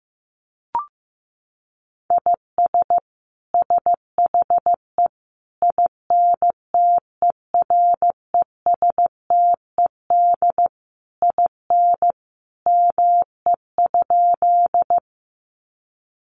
Morse code, heard as a keyed tone, about 14 wpm, 700 Hz. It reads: IS SHE INTERESTED IN ME?
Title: Sentences from Top 900 Words 15wpm